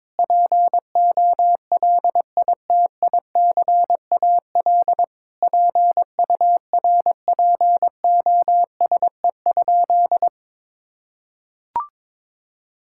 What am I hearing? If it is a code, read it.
Morse code, heard as a keyed tone, about 22 wpm, 700 Hz. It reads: POLITICAL PURPOSE?
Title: Sentences from Top 1000 Words - Review Part 5 - 22wpm